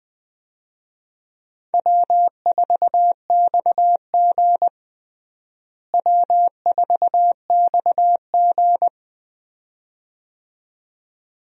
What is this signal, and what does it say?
Morse code, heard as a keyed tone, about 20 wpm, 700 Hz. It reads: W4XG W4XG
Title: USA Call Signs - 4 Characters - Repeated 2x - 20wpm